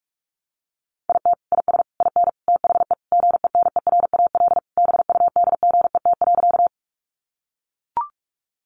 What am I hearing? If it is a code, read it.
Morse code, heard as a keyed tone, about 44 wpm, 700 Hz. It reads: IT IS IN THE GENERAL BUDGET.
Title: Sentences from Top 1200 Words 45wpm